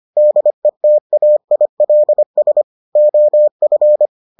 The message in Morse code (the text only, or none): DETAILS OF